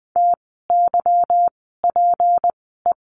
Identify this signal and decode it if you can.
Morse code, heard as a keyed tone, about 20 wpm, spaced as for 14 wpm, 700 Hz. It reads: TYPE
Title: 153 Lesson 25f - Letter p - Words 20wpm